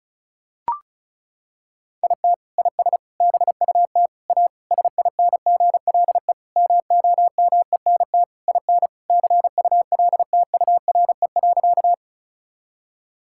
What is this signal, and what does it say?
Morse code, heard as a keyed tone, about 34 wpm, 700 Hz. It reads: IT IS BUT A SINGLE MOMENT IN CULTURE.
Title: Sentences from Top 1000 Words - Review Part 6 - 35wpm